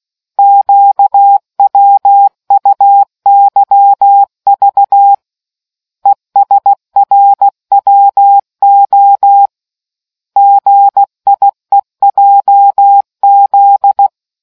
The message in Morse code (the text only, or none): QWUYV ESRWO GIEJZ